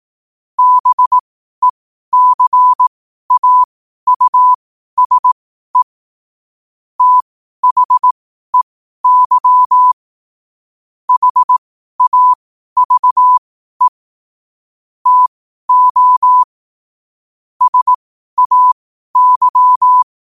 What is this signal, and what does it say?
Morse code, heard as a keyed tone, about 18 wpm, spaced as for 12 wpm, 1000 Hz. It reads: BECAUSE THEY HAVE TO SAY
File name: qotdfiles / 12wpm